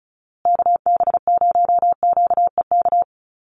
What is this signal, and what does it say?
Morse code, heard as a keyed tone, about 34 wpm, 700 Hz. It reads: KB0QEK
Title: USA Call Signs Repeated 2x 35wpm